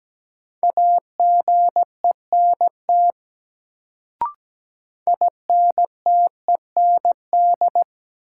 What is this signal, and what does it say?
Morse code, heard as a keyed tone, about 17 wpm, 700 Hz. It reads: AGENT INTEND
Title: Sentences from Top 1600 Words - New Word Drill Alt - 17wpm